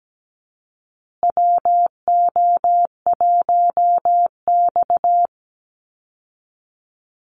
Morse code, heard as a keyed tone, about 17 wpm, 700 Hz. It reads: WO1X
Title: Call Signs Repeated 2x - 17wpm